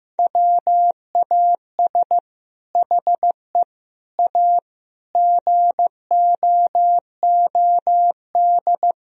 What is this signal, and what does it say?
Morse code, heard as a keyed tone, about 15 wpm, 700 Hz. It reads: WAS HE A GOOD
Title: Sentences from Top 1000 Words - Review Part 4 - 15wpm